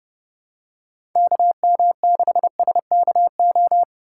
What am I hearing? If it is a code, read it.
Morse code, heard as a keyed tone, about 30 wpm, 700 Hz. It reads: KM6SKO
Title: USA Call Signs - 8x Character Spacing - 30wpm